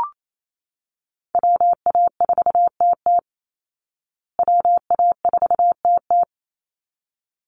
Morse code, heard as a keyed tone, about 28 wpm, 700 Hz. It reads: WA4TT WA4TT
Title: Call Signs Repeated 3x - 28wpm